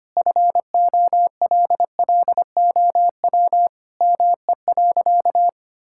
Morse code, heard as a keyed tone, about 25 wpm, 700 Hz. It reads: FOLLOW ME.
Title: Sentences from Top 1000 Words - Review Part 6 - 25wpm